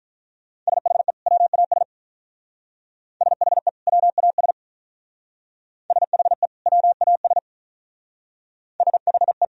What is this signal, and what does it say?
Morse code, heard as a keyed tone, about 45 wpm, 700 Hz. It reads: SHE WAS SHE WAS SHE WAS SHE